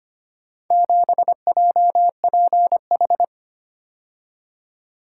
Morse code, heard as a keyed tone, about 25 wpm, 700 Hz. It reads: 7JPH
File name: practice / 25wpm / Four Letter-Number Combinations - 25wpm